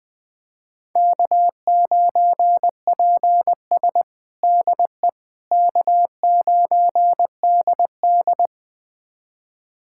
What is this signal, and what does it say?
Morse code, heard as a keyed tone, about 20 wpm, 700 Hz. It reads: K9PS DE K9DD
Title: USA Callsign DE Callsign - 4 Characters - 20wpm